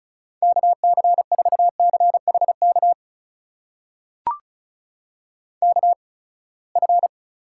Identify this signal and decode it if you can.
Morse code, heard as a keyed tone, about 34 wpm, spaced as for 24 wpm, 700 Hz. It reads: KC4CHK KF
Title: USA Call Signs - 8x Character Spacing - 35wpm